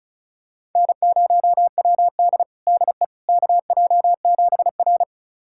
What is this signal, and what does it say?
Morse code, heard as a keyed tone, about 34 wpm, 700 Hz. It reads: N0WD DE KJ7R
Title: USA Callsign DE Callsign - 4 Characters - 35wpm